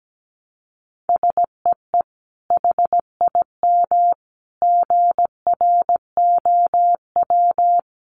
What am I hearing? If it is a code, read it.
Morse code, heard as a keyed tone, about 17 wpm, 700 Hz. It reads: SEE HIM GROW
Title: Sentences from Top 300 Words 17wpm